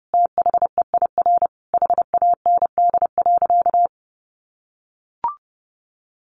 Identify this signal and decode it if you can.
Morse code, heard as a keyed tone, about 30 wpm, 700 Hz. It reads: THEIR HAND.
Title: Sentences from Top 1300 Words 30wpm